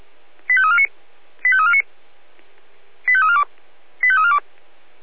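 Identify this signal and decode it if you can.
Five-tone selective call (ZVEI), heard as quick runs of five tones, tones of about 70 ms. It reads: 85349 85349 85342 85342
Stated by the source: SDR-received ZVEI five-tone selective call, code 85340.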